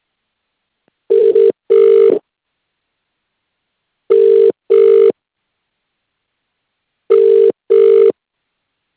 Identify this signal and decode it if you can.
Ringback tone (UK), 400 + 450 Hz.